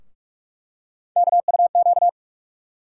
Morse code, heard as a keyed tone, about 47 wpm, 700 Hz. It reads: KUQ
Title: USA Call Sign Postfixes - 3 Letters 45wpm